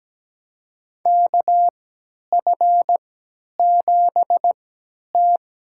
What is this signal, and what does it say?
Morse code, heard as a keyed tone, about 17 wpm, spaced as for 10 wpm, 700 Hz. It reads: KF7T
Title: USA Call Signs - 3x Character Spacing - 17wpm